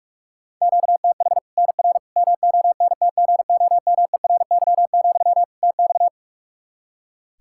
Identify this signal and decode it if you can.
Morse code, heard as a keyed tone, about 46 wpm, 700 Hz. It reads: QTH NR MONTGOMERY, TX
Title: Practice QSO - Exchange 1 - Beginning - 45wpm